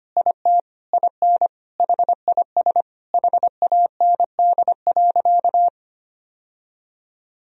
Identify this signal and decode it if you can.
Morse code, heard as a keyed tone, about 25 wpm, 700 Hz. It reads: IT IN HIS HAND.
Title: Sentences from Top 1000 Words - Review Part 2 - 25wpm